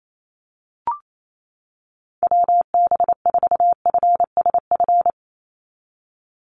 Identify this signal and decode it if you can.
Morse code, heard as a keyed tone, about 28 wpm, 700 Hz. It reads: WB4FSF
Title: Call Signs - 28wpm